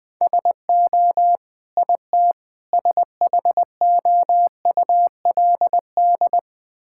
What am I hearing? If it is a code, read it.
Morse code, heard as a keyed tone, about 20 wpm, 700 Hz. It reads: SO IT SHOULD
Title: Sentences with 8-Word Phrases - 20wpm